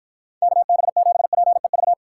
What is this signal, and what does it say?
Morse code, heard as a keyed tone, about 53 wpm, 700 Hz. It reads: KB7PEV